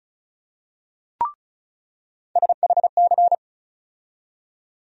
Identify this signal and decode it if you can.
Morse code, heard as a keyed tone, about 34 wpm, 700 Hz. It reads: SHC
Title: USA Call Sign Postfixes - 3 Letters 35wpm